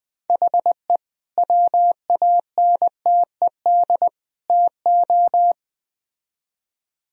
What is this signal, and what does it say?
Morse code, heard as a keyed tone, about 20 wpm, 700 Hz. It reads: HE WANTED TO
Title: Speed-Racing - Sets of 3 Words 17wpm